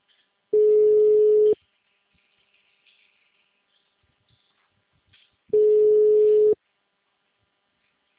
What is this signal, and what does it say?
Ringback tone (Europe, 425 Hz).
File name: out- -2004- - - .51